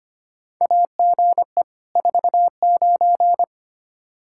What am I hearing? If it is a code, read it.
Morse code, heard as a keyed tone, about 25 wpm, 700 Hz. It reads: AGE 49
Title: Practice QSO - Exchange 3 - More Info - 25wpm